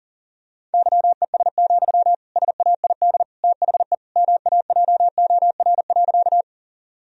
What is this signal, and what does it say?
Morse code, heard as a keyed tone, about 37 wpm, 700 Hz. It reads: YES, SAID THE MAJOR.
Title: Sentences from Top 1000 Words - Review Part 4 - 40wpm